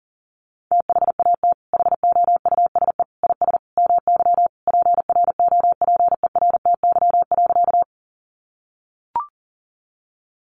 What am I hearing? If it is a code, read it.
Morse code, heard as a keyed tone, about 40 wpm, 700 Hz. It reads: THAT HOUSE IS MY PROPERTY.
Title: Sentences from Top 1000 Words 40wpm